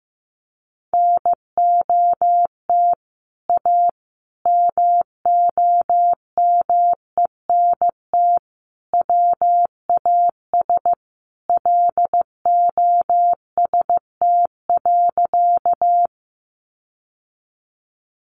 Morse code, heard as a keyed tone, about 15 wpm, 700 Hz. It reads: NOT A MOMENT WAS LOST.
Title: Sentences from Top 1400 Words 15wpm